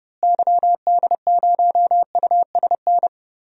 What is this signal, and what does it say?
Morse code, heard as a keyed tone, about 31 wpm, 700 Hz. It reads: YD0USN
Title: DXCC-100 Call Signs - 6 Characters - Repeated 3x - 30wpm